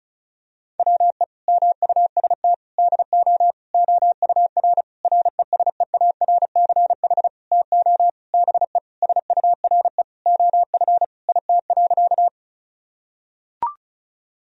Morse code, heard as a keyed tone, about 36 wpm, 700 Hz. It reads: WE MUST DO OUR RESEARCH TO BE SURE OF IT.
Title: Sentences from Top 1000 Words - Review Part 4 - 35wpm